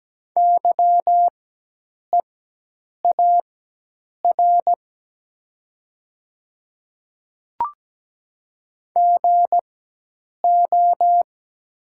Morse code, heard as a keyed tone, about 17 wpm, spaced as for 8 wpm, 700 Hz. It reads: YEAR GO
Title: 4x Character Spacing - Top 200 Words - 17wpm